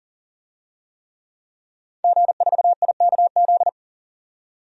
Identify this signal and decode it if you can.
Morse code, heard as a keyed tone, about 40 wpm, 700 Hz. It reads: G4IKZ
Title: DXCC-100 Call Signs - 5 Characters - 40wpm